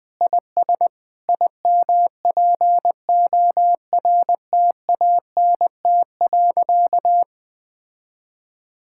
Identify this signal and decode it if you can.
Morse code, heard as a keyed tone, about 20 wpm, 700 Hz. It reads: IS IMPORTANT.
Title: Sentences with 8-Word Phrases - 20wpm